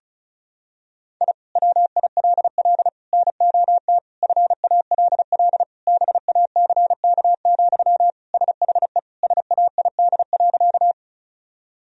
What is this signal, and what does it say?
Morse code, heard as a keyed tone, about 34 wpm, 700 Hz. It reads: I WILL NOT FALL BACK, SHE SAID.
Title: Sentences from Top 1000 Words - Review Part 2 - 35wpm